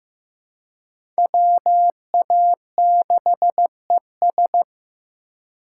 Morse code, heard as a keyed tone, about 15 wpm, 700 Hz. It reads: WA6ES
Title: USA Call Signs - 5 Characters - 2x Character Spacing - 15wpm